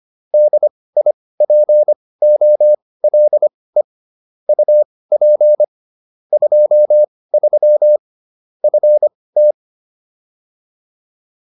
Morse code, heard as a keyed tone, about 25 wpm, spaced as for 18 wpm, 600 Hz. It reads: DIPOLE UP 23 FT
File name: QSO308 18